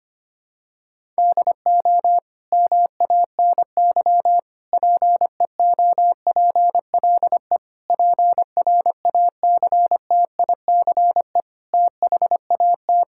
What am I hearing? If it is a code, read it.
Morse code, heard as a keyed tone, about 25 wpm, 700 Hz. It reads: DO MANY PEOPLE PRACTICE THAT
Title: Sentences from Top 1000 Words - Review Part 6 - 25wpm